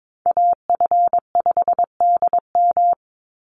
Morse code, heard as a keyed tone, about 22 wpm, 700 Hz. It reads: AF5DM
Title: Call Signs - 22wpm